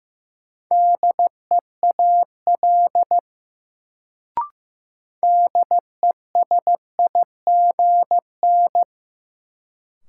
Morse code, heard as a keyed tone, about 15 wpm, 700 Hz. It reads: DEAL DESIGN
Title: Sentences from Top 800 Words - New Word Drill - 15wpm